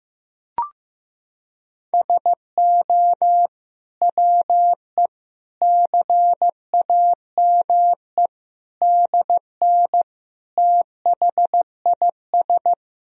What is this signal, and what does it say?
Morse code, heard as a keyed tone, about 15 wpm, 700 Hz. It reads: SO WE CAME DN THIS
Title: Sentences from Top 400 Words and Abbreviations 15wpm